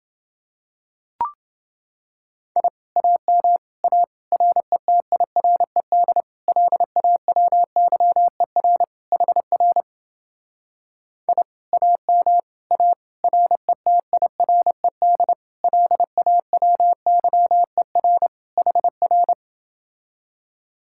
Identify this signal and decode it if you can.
Morse code, heard as a keyed tone, about 30 wpm, 700 Hz. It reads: I AM A RETIRED LAWYER HR I AM A RETIRED LAWYER HR